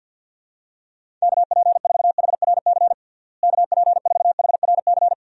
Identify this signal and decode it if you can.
Morse code, heard as a keyed tone, about 53 wpm, 700 Hz. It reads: KP4HRC KP4HRC
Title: USA Call Signs - 6 Characters - Repeated 2x - 50wpm